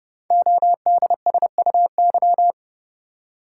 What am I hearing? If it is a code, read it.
Morse code, heard as a keyed tone, about 31 wpm, 690 Hz. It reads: ODSUY
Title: Code Groups - 5 Letters - 30wpm